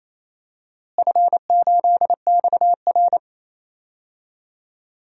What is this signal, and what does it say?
Morse code, heard as a keyed tone, about 28 wpm, 700 Hz. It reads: F8XR
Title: DXCC-100 Call Signs - 4 Characters - Repeated 2x - 28wpm